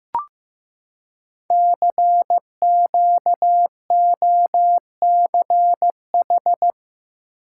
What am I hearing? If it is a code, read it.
Morse code, heard as a keyed tone, about 15 wpm, 700 Hz. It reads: CQOCH
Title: Code Groups - 5 Letter:Numbers - 15wpm